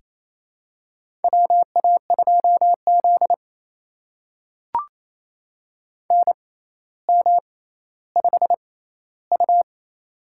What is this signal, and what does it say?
Morse code, heard as a keyed tone, about 28 wpm, spaced as for 14 wpm, 700 Hz. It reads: WA2Z NM5U
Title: USA Call Signs - 4 Characters - 6x Character Spacing - 28wpm